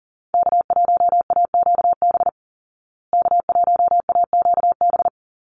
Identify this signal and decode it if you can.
Morse code, heard as a keyed tone, about 37 wpm, 700 Hz. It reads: K1AQB K1AQB
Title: Call Signs Repeated 2x - 40wpm